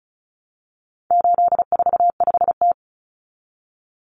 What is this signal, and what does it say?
Morse code, heard as a keyed tone, about 34 wpm, 700 Hz. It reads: 845T
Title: Common Cut Numbers - 4-Digit 35wpm